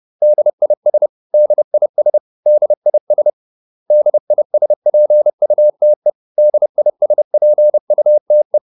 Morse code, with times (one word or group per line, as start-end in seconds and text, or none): 0.2-1.1 DIS
1.3-2.2 DIS
2.5-3.3 DIS
3.9-6.1 DISPUTE
6.4-8.6 DISPUTE